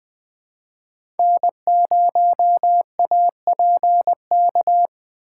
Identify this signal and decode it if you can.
Morse code, heard as a keyed tone, about 20 wpm, 700 Hz. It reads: N0APK
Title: USA Call Signs - 5 Characters - 3x Character Spacing - 20wpm